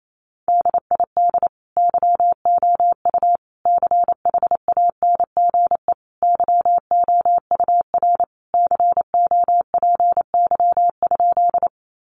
Morse code, heard as a keyed tone, about 28 wpm, 700 Hz. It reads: DID YOU CHANGE YOUR COPY?
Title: Sentences from Top 1400 Words 28wpm